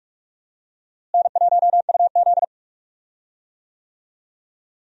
Morse code, heard as a keyed tone, about 46 wpm, 700 Hz. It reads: N1UZ
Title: USA Call Signs - 4 Characters - Repeated 2x - 45wpm